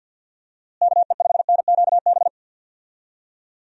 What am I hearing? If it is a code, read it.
Morse code, heard as a keyed tone, about 52 wpm, 700 Hz. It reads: KE5NQB
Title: USA Call Signs - 6 Characters - Repeated 2x - 50wpm